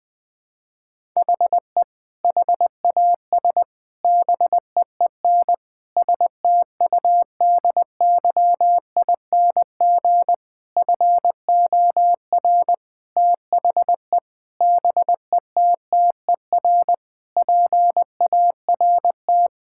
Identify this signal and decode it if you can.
Morse code, heard as a keyed tone, about 20 wpm, 700 Hz. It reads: HE HAS BEEN STUDYING FOR THE BETTER PART
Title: Sentences with 7-Word Phrases - Spoken Beginning - 20wpm